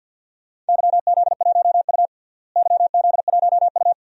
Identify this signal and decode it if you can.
Morse code, heard as a keyed tone, about 51 wpm, 700 Hz. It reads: YZ1U YZ1U
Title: DXCC-100 Call Signs - 4 Characters - Repeated 3x - 50wpm